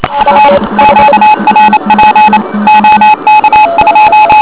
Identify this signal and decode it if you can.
Morse code, heard as a keyed tone, about 28 wpm, 850 Hz, with noise. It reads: U QRP OK2
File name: OK QRP 2007 / REC31